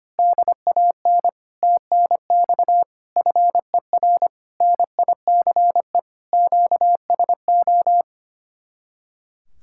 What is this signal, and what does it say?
Morse code, heard as a keyed tone, about 24 wpm, 700 Hz. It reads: DAN TNX FER NICE QSO
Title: Speed-Racing Common QSO Phrases Encore - 25wpm